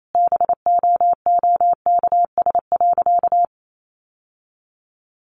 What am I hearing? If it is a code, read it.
Morse code, heard as a keyed tone, about 28 wpm, 700 Hz. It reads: BOOKS.
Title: Sentences from Top 1200 Words 28wpm